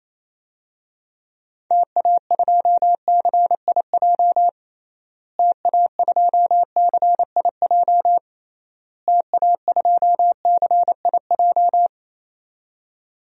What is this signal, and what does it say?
Morse code, heard as a keyed tone, about 28 wpm, 700 Hz. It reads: TA2CIJ TA2CIJ TA2CIJ
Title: DXCC-100 Call Signs - 6 Characters - Repeated 3x - 28wpm